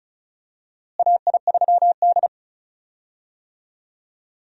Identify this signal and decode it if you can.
Morse code, heard as a keyed tone, about 36 wpm, 700 Hz. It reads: AI3D
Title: USA Call Signs - 4 Characters - Repeated 3x - 35wpm